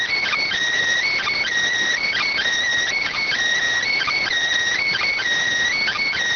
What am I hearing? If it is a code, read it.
Emergency Alert System (SAME) tones, heard as harsh buzzing bursts. It is heard from 1.0 to 5.2 s.